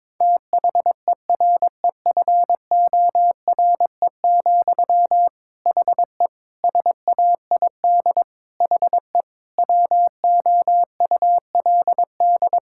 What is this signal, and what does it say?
Morse code, heard as a keyed tone, about 22 wpm, 700 Hz. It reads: THEREFORE, HE SAID HE WOULD
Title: Sentences from Top 1000 Words - Review Part 5 - 22wpm